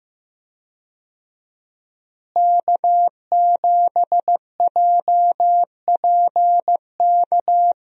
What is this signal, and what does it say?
Morse code, heard as a keyed tone, about 15 wpm, 700 Hz. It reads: K7JPK
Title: USA Call Signs - 5 Characters - Repeated 2x - 15wpm